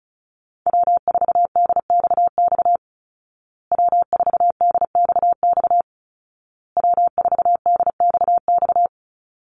Morse code, heard as a keyed tone, about 34 wpm, 700 Hz. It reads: W4DXX W4DXX W4DXX
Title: Call Signs Repeated 2x - 35wpm